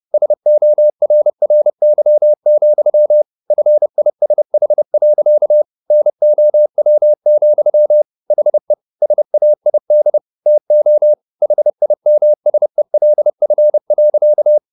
Morse code, heard as a keyed tone, about 30 wpm, 600 Hz. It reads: SORRY, FISH. NOW, HE SAID TO HIMSELF.